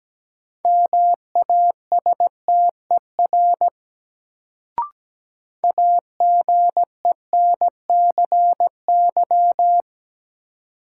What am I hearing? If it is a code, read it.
Morse code, heard as a keyed tone, about 17 wpm, 700 Hz. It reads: MASTER AGENCY
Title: Sentences from Top 1300 Words - New Word Drill - 17wpm